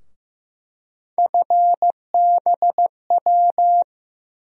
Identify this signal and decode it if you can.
Morse code, heard as a keyed tone, about 15 wpm, 700 Hz. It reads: FBW
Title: USA Call Sign Postfixes - 3 Letters 15wpm